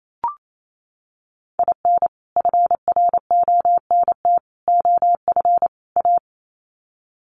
Morse code, heard as a keyed tone, about 28 wpm, 700 Hz. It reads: IN FRONT OF A
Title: Sets of 4 Words 28wpm